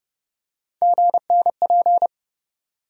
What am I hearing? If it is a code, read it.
Morse code, heard as a keyed tone, about 30 wpm, 700 Hz. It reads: GNP
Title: USA Call Sign Postfixes - 3 Letters 30wpm